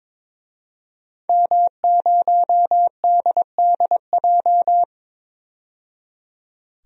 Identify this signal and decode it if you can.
Morse code, heard as a keyed tone, about 22 wpm, 700 Hz. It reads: M0DDJ
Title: DXCC-100 Call Signs - 5 Characters - 22wpm